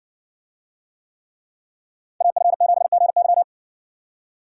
Morse code, heard as a keyed tone, about 60 wpm, 700 Hz. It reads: AF7GY